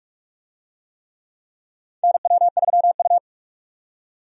Morse code, heard as a keyed tone, about 44 wpm, 700 Hz. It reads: NW3U
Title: USA Call Signs - 4 Characters - 45wpm